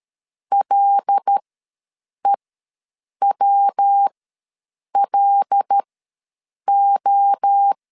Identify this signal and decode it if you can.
Morse code, heard as a keyed tone, about 13 wpm, spaced as for 7 wpm, 800 Hz. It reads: LEWLO
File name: LEZIONI / 5 LBVC / 5 L ws 12wpm 2958